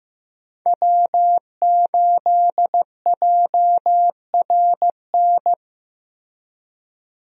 Morse code, heard as a keyed tone, about 16 wpm, 700 Hz. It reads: W8JRN
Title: USA Call Signs - 5 Characters - Repeated 3x - 15wpm